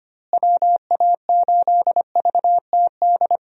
Morse code, heard as a keyed tone, about 25 wpm, 700 Hz. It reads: WA8VTD
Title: USA Call Signs - 6 Characters - Repeated 2x - 25wpm